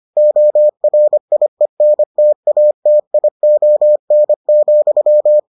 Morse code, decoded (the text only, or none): ORIENTATION,